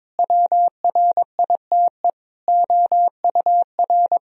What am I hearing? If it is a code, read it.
Morse code, heard as a keyed tone, about 22 wpm, 700 Hz. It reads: WRITE OUR
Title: Sentences from Top 1000 Words - Review Part 3 - 22wpm